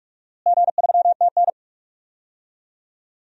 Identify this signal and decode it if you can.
Morse code, heard as a keyed tone, about 46 wpm, 700 Hz. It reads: G3TN
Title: DXCC-100 Call Signs - 4 Characters - Repeated 3x - 45wpm